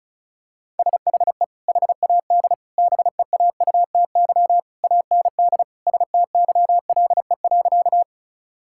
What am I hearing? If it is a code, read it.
Morse code, heard as a keyed tone, about 36 wpm, 690 Hz. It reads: SHE HAD BEAUTY AND STYLE.
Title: Sentences from Top 1000 Words - Review Part 5 - 35wpm